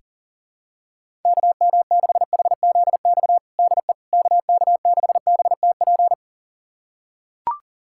Morse code, heard as a keyed tone, about 37 wpm, 700 Hz. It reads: KM6HZX DE KK6BTP
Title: USA Callsign DE Callsign 40wpm